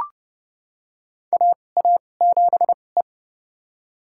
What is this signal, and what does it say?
Morse code, heard as a keyed tone, about 30 wpm, spaced as for 22 wpm, 700 Hz. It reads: AA7E
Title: USA Call Signs - 4 Characters - 2x Character Spacing - 30wpm